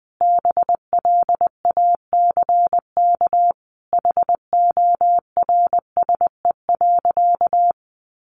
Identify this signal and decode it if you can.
Morse code, heard as a keyed tone, about 20 wpm, 700 Hz. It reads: BLACK HORSE.
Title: Sentences from Top 1500 Words 20wpm